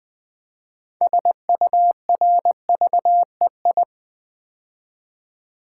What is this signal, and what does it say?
Morse code, heard as a keyed tone, about 20 wpm, 700 Hz. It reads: SURVEI
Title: Build A Word - 6 Letters At A Time - 12-Letter Words - 20wpm